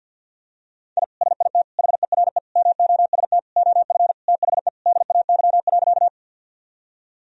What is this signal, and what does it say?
Morse code, heard as a keyed tone, about 49 wpm, 700 Hz. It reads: I SIT HERE MOST OF THE DAY.
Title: Sentences from Top 1000 Words - Review Part 1 - 50wpm